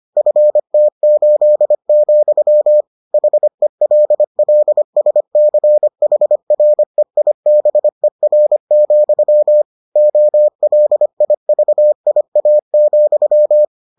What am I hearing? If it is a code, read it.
Morse code, heard as a keyed tone, about 25 wpm, 600 Hz. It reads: FT8, HELLSCHREIBER, OLIVIA,